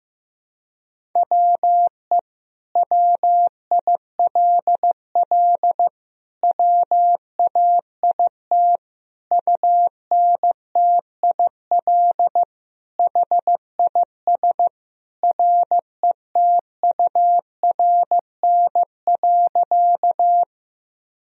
Morse code, heard as a keyed tone, about 15 wpm, 700 Hz. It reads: WE WILL WAIT UNTIL HIS RETURN.
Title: Sentences from Top 1000 Words - Review Part 1 - 15wpm